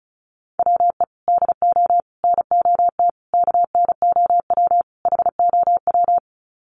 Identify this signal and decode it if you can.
Morse code, heard as a keyed tone, about 34 wpm, 700 Hz. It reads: WE DO NOT KNOW HOW
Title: Sets of 5 Words Encore 35wpm